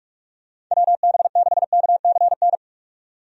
Morse code, heard as a keyed tone, about 46 wpm, 700 Hz. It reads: WB6KCN
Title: USA Call Signs - 6 Characters - 45wpm